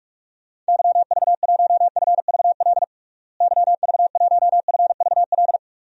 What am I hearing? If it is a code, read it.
Morse code, heard as a keyed tone, about 46 wpm, 690 Hz. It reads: YV1FVL YV1FVL
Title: DXCC-100 Call Signs - 6 Characters - Repeated 2x - 45wpm